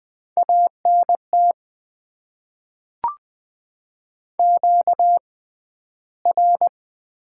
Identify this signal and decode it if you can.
Morse code, heard as a keyed tone, about 20 wpm, spaced as for 12 wpm, 700 Hz. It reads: ANT QR
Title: Top 20 QSO Elements - 6x Character Spacing - 20wpm